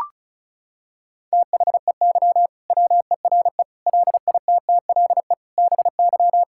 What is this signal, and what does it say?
Morse code, heard as a keyed tone, about 34 wpm, 700 Hz. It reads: THEY WERE LITTLE BY
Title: Sentences from Top 1600 Words 35wpm